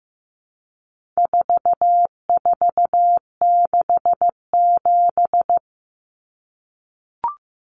Morse code, heard as a keyed tone, about 15 wpm, 700 Hz. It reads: 4467
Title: Numbers - 4-Digit 15wpm